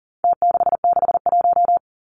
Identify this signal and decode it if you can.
Morse code, heard as a keyed tone, about 40 wpm, 700 Hz. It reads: T661
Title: Numbers - Contest Style - 4-Digit 40wpm